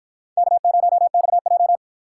Morse code, heard as a keyed tone, about 53 wpm, 700 Hz. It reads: K0XJ